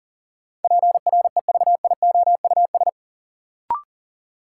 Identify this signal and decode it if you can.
Morse code, heard as a keyed tone, about 40 wpm, 700 Hz. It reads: PREVIOUS
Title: Sentences from Top 1500 Words - New Word Drill - 40wpm